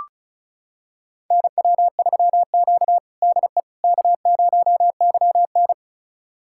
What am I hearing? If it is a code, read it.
Morse code, heard as a keyed tone, about 36 wpm, 700 Hz. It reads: NW3Q DE K0YN
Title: USA Callsign DE Callsign - 4 Characters - 35wpm